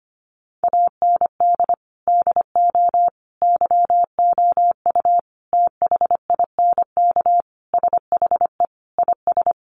Morse code, heard as a keyed tone, about 25 wpm, 700 Hz. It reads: AND DO YOU THINK SHE IS
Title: Sentences from Top 600 Words 25wpm